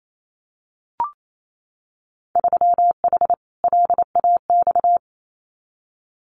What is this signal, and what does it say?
Morse code, heard as a keyed tone, about 28 wpm, 700 Hz. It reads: 3H LAX
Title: Winter Field Day Exchanges 28wpm